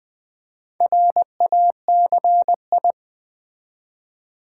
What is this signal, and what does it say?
Morse code, heard as a keyed tone, about 20 wpm, 700 Hz. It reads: RACI
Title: Build A Word - 2 Letters At A Time - 6-Letter Words - 20wpm